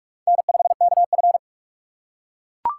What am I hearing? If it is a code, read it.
Morse code, heard as a keyed tone, about 42 wpm, 700 Hz. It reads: N5KF
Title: USA Call Signs - 6x Character Spacing - 45wpm